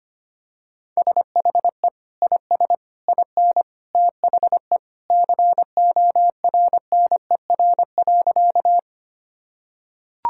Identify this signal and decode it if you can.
Morse code, heard as a keyed tone, about 25 wpm, 700 Hz. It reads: SHE IS IN THE CORNER.
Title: Sentences from Top 1600 Words 25wpm